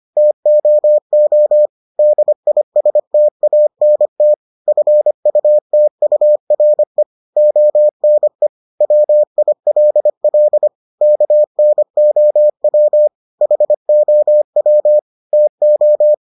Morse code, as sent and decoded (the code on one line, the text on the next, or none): - --- --- / -.. .. ... - .- -. - / ..-. ..- - ..- .-. / --- -. / .-- .. .-.. .-.. / -.- -. --- .-- / .... --- .-- / - ---
TOO DISTANT FUTURE ONE WILL KNOW HOW TO